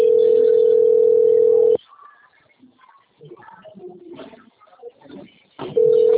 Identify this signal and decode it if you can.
Ringback tone (North America), 440 + 480 Hz.